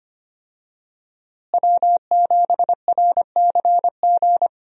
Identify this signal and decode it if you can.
Morse code, heard as a keyed tone, about 25 wpm, 700 Hz. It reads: W7RCG
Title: USA Call Signs - 5 Characters - Repeated 2x - 25wpm